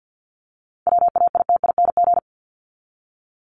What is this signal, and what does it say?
Morse code, heard as a keyed tone, about 52 wpm, 700 Hz. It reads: WAITING
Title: ING Suffix - Top 100 Words with Base Words 50wpm